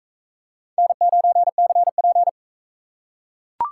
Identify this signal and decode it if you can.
Morse code, heard as a keyed tone, about 44 wpm, 700 Hz. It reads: N9CP